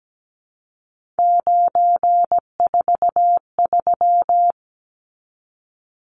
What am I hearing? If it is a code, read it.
Morse code, heard as a keyed tone, about 17 wpm, 700 Hz. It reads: 943
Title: Numbers - 3-Digit 17wpm